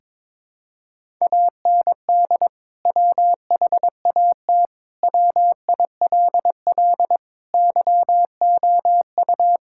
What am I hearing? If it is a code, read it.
Morse code, heard as a keyed tone, about 22 wpm, 700 Hz. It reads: AND WHAT WILL YOU
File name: practice / 22wpm / Sentences with 2-Word Phrases - 22wpm